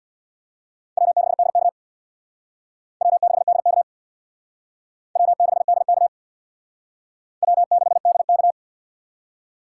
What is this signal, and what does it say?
Morse code, heard as a keyed tone, about 58 wpm, 700 Hz. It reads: W6DK W6DK W6DK W6DK
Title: Speed Racing - USA 1x2 Call Signs 50wpm